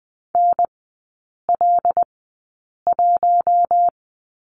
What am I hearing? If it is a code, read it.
Morse code, heard as a keyed tone, about 20 wpm, spaced as for 8 wpm, 700 Hz. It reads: NL1
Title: 127 Lesson 21e - Number 6 - Three Characters 20wpm